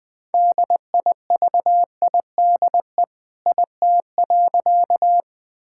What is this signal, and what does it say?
Morse code, heard as a keyed tone, about 21 wpm, 700 Hz. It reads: DIVIDE IT.
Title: Sentences from Top 1600 Words 20wpm